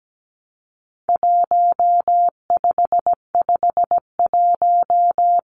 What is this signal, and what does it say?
Morse code, heard as a keyed tone, about 17 wpm, 700 Hz. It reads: 1551
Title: Numbers - Contest Style - 4-Digit 17wpm